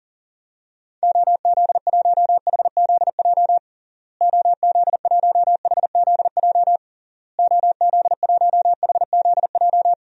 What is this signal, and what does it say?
Morse code, heard as a keyed tone, about 40 wpm, 700 Hz. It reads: OZ1HZJ OZ1HZJ OZ1HZJ
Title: DXCC-100 Call Signs - 6 Characters - Repeated 3x - 40wpm